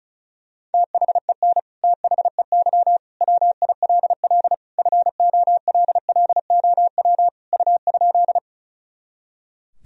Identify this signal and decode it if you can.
Morse code, heard as a keyed tone, about 34 wpm, 700 Hz. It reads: THEN THEY WILL FOLLOW U?
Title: Sentences from Top 200 Words and Abbreviations 35wpm